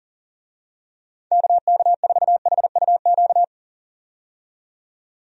Morse code, heard as a keyed tone, about 40 wpm, 700 Hz. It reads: KK4HUQ
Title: USA Call Signs - 6 Characters - 40wpm